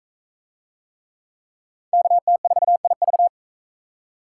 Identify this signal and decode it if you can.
Morse code, heard as a keyed tone, about 39 wpm, 700 Hz. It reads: KT4IV